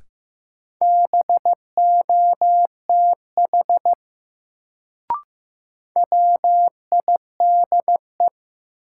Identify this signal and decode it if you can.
Morse code, heard as a keyed tone, about 15 wpm, 700 Hz. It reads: BOTH WIDE
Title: Sets of 4 Words - New Word Drill - 15wpm